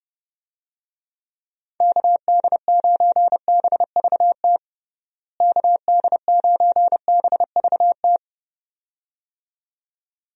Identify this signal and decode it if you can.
Morse code, heard as a keyed tone, about 30 wpm, 700 Hz. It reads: KD9BVT KD9BVT
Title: USA Call Signs - 6 Characters - Repeated 2x - 30wpm